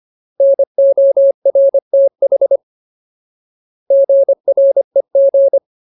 Morse code, heard as a keyed tone, about 25 wpm, 550 Hz. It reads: NORTH GREG
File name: Ultimate Word List / 25WPM Ultimate